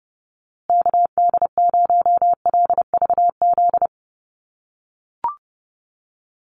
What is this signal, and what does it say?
Morse code, heard as a keyed tone, about 30 wpm, 700 Hz. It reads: KD0LVZ
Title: USA Call Signs Repeated 3x 30wpm